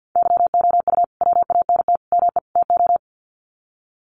Morse code, heard as a keyed tone, about 48 wpm, 700 Hz. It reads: YOU WANT ME TO
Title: Sets of 4 Words 50wpm